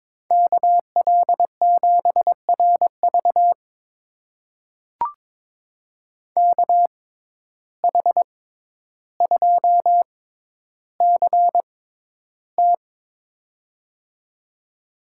Morse code, heard as a keyed tone, about 23 wpm, spaced as for 11 wpm, 700 Hz. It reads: KL7RV KH2CT
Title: USA Call Signs - 5 Characters - 6x Character Spacing - 22wpm